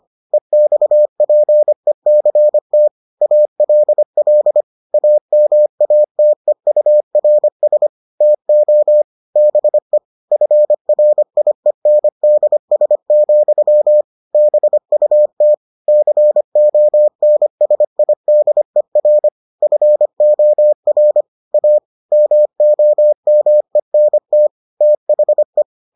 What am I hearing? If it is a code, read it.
Morse code, heard as a keyed tone, about 25 wpm, 600 Hz. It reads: EXPECT ALL AMATEURS TO BE FRIENDS, BUT CONSIDER FOR A MOMENT THE